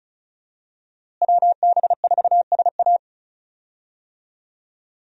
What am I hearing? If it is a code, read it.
Morse code, heard as a keyed tone, about 36 wpm, 700 Hz. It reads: WB4SA
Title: USA Call Signs - 5 Characters - Repeated 2x - 35wpm